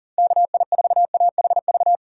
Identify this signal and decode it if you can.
Morse code, heard as a keyed tone, about 40 wpm, 700 Hz. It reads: KI4AHV
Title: USA Call Signs - 6 Characters - 3x Character Spacing - 40wpm